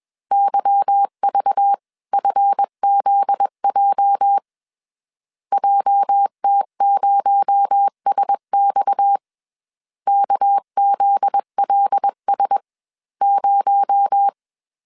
Morse code, heard as a keyed tone, about 21 wpm, spaced as for 18 wpm, 800 Hz. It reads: YVFZJ JT0SX KZLS0
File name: CW / DET / 1 9 1500C S C 20wpm 2310